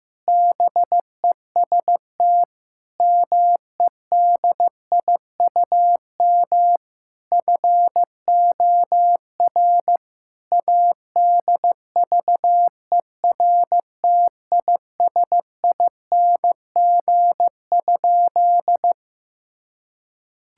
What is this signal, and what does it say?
Morse code, heard as a keyed tone, about 16 wpm, 700 Hz. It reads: BEST MEDIUM FOR ADVERTISING?
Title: Sentences from Top 1000 Words - Review Part 6 - 15wpm